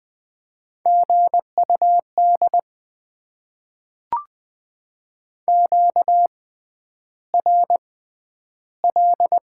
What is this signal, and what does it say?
Morse code, heard as a keyed tone, about 20 wpm, spaced as for 10 wpm, 700 Hz. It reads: GUD QRL
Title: Top 20 QSO Elements - 6x Character Spacing - 20wpm